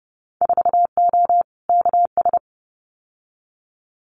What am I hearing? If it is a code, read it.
Morse code, heard as a keyed tone, about 30 wpm, 700 Hz. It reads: 4O KS
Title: Winter Field Day Exchanges Encore 30wpm